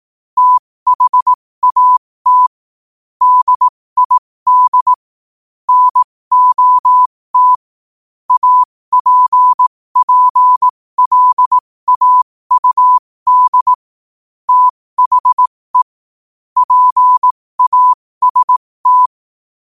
Morse code, heard as a keyed tone, about 18 wpm, spaced as for 15 wpm, 1000 Hz. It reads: THAT DID NOT APPLAUD THE PAST
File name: qotdfiles / 15wpm